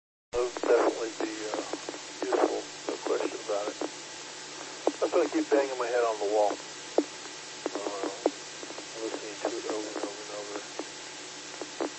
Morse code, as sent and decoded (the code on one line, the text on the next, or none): none